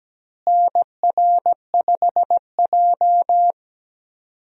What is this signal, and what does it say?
Morse code, heard as a keyed tone, about 17 wpm, 700 Hz. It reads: NR5J
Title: USA Call Signs - 4 Characters - Repeated 2x - 17wpm